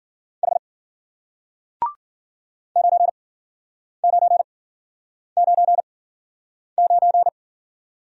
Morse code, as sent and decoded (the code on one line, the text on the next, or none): .... / ----. ----. ----. ----.
H 9999